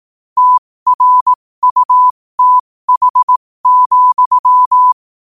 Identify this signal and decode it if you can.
Morse code, heard as a keyed tone, about 18 wpm, spaced as for 15 wpm, 1000 Hz. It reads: TRUTH,
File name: qotdfiles / 15wpm